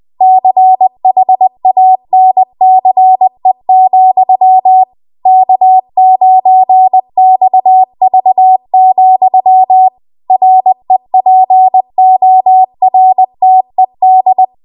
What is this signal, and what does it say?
Morse code, heard as a keyed tone, about 20 wpm, 750 Hz. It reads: CHANCE, K9XV, REPORTED